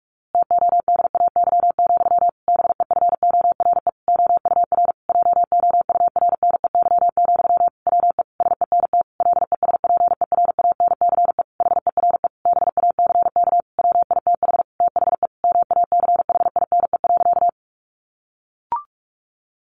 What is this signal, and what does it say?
Morse code, heard as a keyed tone, about 44 wpm, 700 Hz. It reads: TODAY, BEFORE OUR JOURNEY, WE SENT LESPERANCE HERE BACK WITH THE MACHINE.
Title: A Sound Of Thunder 45wpm